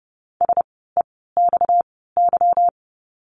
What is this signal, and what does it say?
Morse code, heard as a keyed tone, about 30 wpm, spaced as for 17 wpm, 700 Hz. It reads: SEXY
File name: course / 30:10wpm / 230 Lesson 37f - Letter x - Words 30wpm